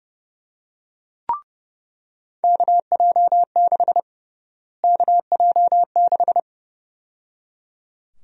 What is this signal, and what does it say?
Morse code, heard as a keyed tone, about 31 wpm, 700 Hz. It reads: KJ6 KJ6
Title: USA Call Sign Prefixes - Repeated 2x - 30wpm